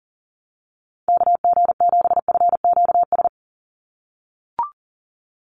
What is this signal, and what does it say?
Morse code, heard as a keyed tone, about 40 wpm, 700 Hz. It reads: KG7FQS
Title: USA Call Signs 40wpm